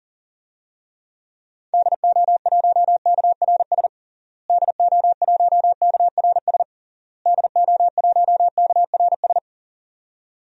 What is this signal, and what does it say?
Morse code, heard as a keyed tone, about 40 wpm, 700 Hz. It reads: DO1KRS DO1KRS DO1KRS
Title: DXCC-100 Call Signs - 6 Characters - Repeated 3x - 40wpm